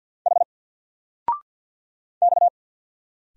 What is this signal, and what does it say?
Morse code, heard as a keyed tone, about 50 wpm, 700 Hz. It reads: H X
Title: Single Letter-Number 50wpm